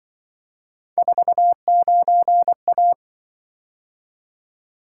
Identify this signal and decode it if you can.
Morse code, heard as a keyed tone, about 24 wpm, 700 Hz. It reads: 49A
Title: Speed-Racing - Three Letter-Number Combinations - 20wpm